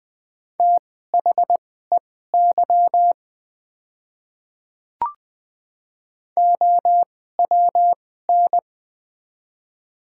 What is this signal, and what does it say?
Morse code, heard as a keyed tone, about 20 wpm, spaced as for 15 wpm, 700 Hz. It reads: THEY OWN
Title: 2x Character Spacing - Top 300 Words - 20wpm